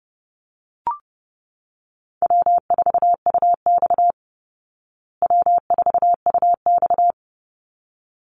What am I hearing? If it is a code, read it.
Morse code, heard as a keyed tone, about 30 wpm, 700 Hz. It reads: W4UX W4UX
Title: Easy Call Signs Repeated 2x - 30wpm